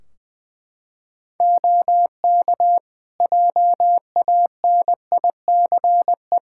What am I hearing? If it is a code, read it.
Morse code, heard as a keyed tone, about 21 wpm, 700 Hz. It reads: OK JANICE
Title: Practice QSO - Exchange 3 - More Info - 20wpm